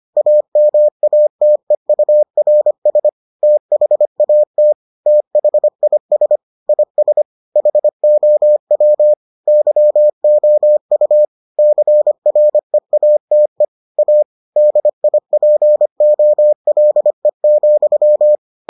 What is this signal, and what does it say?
Morse code, heard as a keyed tone, about 25 wpm, 600 Hz. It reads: AMATEURS THAT THIS IS HOW YOU CREATE A DIPOLE,